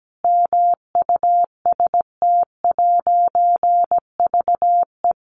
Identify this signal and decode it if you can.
Morse code, heard as a keyed tone, about 17 wpm, 700 Hz. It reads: MUST'VE
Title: A Sound Of Thunder 17wpm